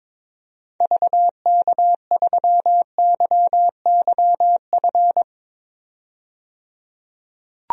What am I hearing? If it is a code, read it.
Morse code, heard as a keyed tone, about 22 wpm, 700 Hz. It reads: VK3YYF
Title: DXCC-100 Call Signs - 6 Characters - 22wpm